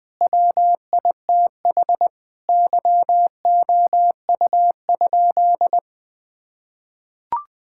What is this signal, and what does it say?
Morse code, heard as a keyed tone, about 20 wpm, 700 Hz. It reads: WITH YOU?
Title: Sentences from Top 1000 Words - Review Part 5 - 20wpm